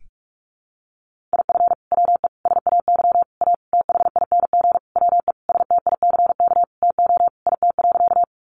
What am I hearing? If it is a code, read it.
Morse code, heard as a keyed tone, about 44 wpm, 700 Hz. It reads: IF WE SAY A THING WE STICK TO IT.
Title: Sentences from Top 1200 Words 45wpm